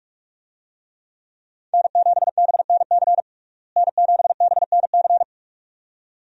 Morse code, heard as a keyed tone, about 47 wpm, 700 Hz. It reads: N7BNC N7BNC
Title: USA Call Signs - 5 Characters - Repeated 2x - 45wpm